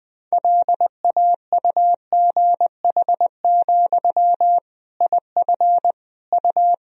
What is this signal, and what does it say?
Morse code, heard as a keyed tone, about 21 wpm, 700 Hz. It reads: LAUGH, IF U
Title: Sentences from Top 500 Words and Abbreviations 20wpm